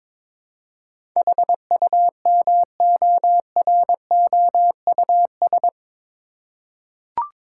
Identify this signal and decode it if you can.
Morse code, heard as a keyed tone, about 22 wpm, 700 Hz. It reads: HUMOROUS
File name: practice / 22wpm / Eight Letter Words - 22wpm